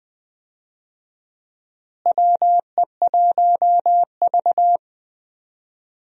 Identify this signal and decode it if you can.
Morse code, heard as a keyed tone, about 20 wpm, 700 Hz. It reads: WE1V
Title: USA Call Signs - 4 Characters - Repeated 3x - 20wpm